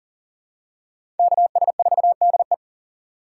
Morse code, heard as a keyed tone, about 39 wpm, 700 Hz. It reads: KS4DE
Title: USA Call Signs - 5 Characters - 3x Character Spacing - 40wpm